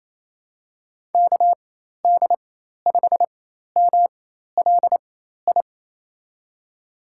Morse code, heard as a keyed tone, about 27 wpm, spaced as for 13 wpm, 700 Hz. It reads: KD5MLI
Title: USA Call Signs - 4x Character Spacing - 28wpm